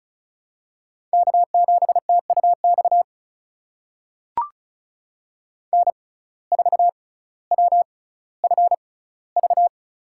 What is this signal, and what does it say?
Morse code, heard as a keyed tone, about 36 wpm, spaced as for 18 wpm, 700 Hz. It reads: K7TUX N4WFV
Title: USA Call Signs - 5 Characters - 6x Character Spacing - 35wpm